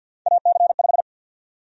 Morse code, heard as a keyed tone, about 53 wpm, 700 Hz. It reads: AC5
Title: USA Call Sign Prefixes - 50wpm